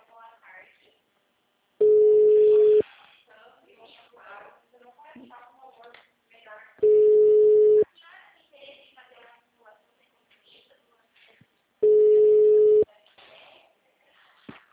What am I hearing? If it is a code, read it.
Ringback tone (Europe, 425 Hz), with speech.
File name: out- -2006- - - .85699